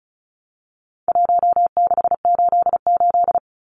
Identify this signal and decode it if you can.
Morse code, heard as a keyed tone, about 34 wpm, 700 Hz. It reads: 1688
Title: Numbers - Contest Style - 4-Digit 35wpm